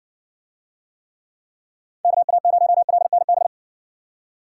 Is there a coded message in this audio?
Yes: Morse code from 2.0 to 3.5 s, heard as a keyed tone, about 55 wpm, 700 Hz. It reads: KA0LNB